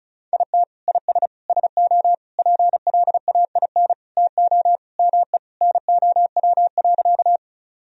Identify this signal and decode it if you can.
Morse code, heard as a keyed tone, about 36 wpm, 700 Hz. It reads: IT IS SO PLAIN TO ME NOW.
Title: Sentences from Top 1000 Words - Review Part 2 - 35wpm